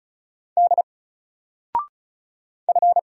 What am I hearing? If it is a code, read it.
Morse code, heard as a keyed tone, about 35 wpm, 700 Hz. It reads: D F
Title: Single Letter-Number Alt 35wpm